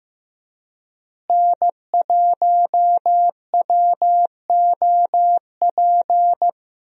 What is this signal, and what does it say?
Morse code, heard as a keyed tone, about 15 wpm, 700 Hz. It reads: N1WOP
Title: USA Call Signs - 5 Characters - Repeated 3x - 15wpm